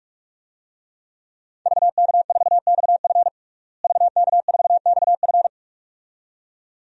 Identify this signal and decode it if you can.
Morse code, heard as a keyed tone, about 46 wpm, 700 Hz. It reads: VK4XF VK4XF
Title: DXCC-100 Call Signs - 5 Characters - Repeated 2x - 45wpm